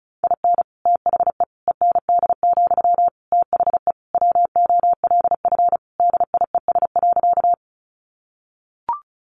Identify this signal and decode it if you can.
Morse code, heard as a keyed tone, about 34 wpm, 700 Hz. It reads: IN THE END, THE WOLF DIES.
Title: Additional Easy Phrases 35wpm